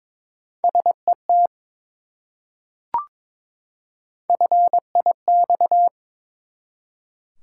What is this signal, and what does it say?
Morse code, heard as a keyed tone, about 22 wpm, 700 Hz. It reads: SET FIX
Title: Three Letter Words - 22wpm